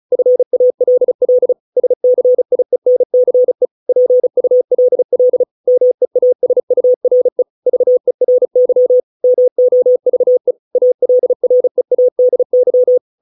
Morse code, heard as a keyed tone, about 34 wpm, 500 Hz. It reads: FALL SCIENCE PULL MEASURE VERY MOVE ALREADY